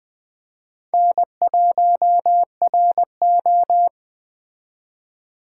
Morse code, heard as a keyed tone, about 20 wpm, 700 Hz. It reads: N1RO